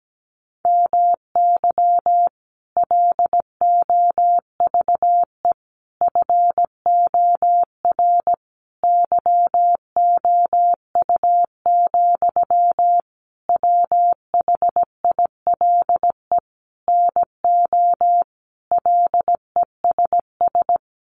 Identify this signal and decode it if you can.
Morse code, heard as a keyed tone, about 17 wpm, 700 Hz. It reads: MY LOVE FOR YOU, WHILE NO LESS
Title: Sentences from Top 1400 Words 17wpm